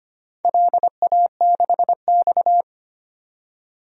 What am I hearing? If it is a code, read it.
Morse code, heard as a keyed tone, about 25 wpm, 700 Hz. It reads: LA6X
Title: DXCC-100 Call Signs - 4 Characters - 25wpm